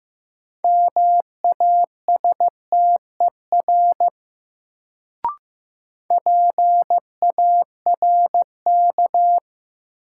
Morse code, heard as a keyed tone, about 15 wpm, 700 Hz. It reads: MASTER PARK
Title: Sentences from Top 1300 Words - New Word Drill - 15wpm